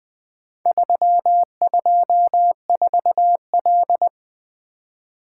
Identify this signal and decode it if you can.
Morse code, heard as a keyed tone, about 20 wpm, 700 Hz. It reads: 324L
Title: Four Letter-Number Combinations - 20wpm